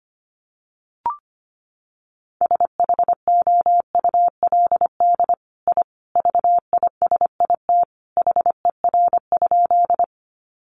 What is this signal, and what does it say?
Morse code, heard as a keyed tone, about 25 wpm, 700 Hz. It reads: SHOULD I VISIT HER?
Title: Sentences from Top 900 Words 25wpm